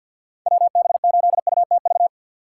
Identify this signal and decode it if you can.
Morse code, heard as a keyed tone, about 51 wpm, 700 Hz. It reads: WB8UTV
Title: USA Call Signs - 6 Characters - 3x Character Spacing - 50wpm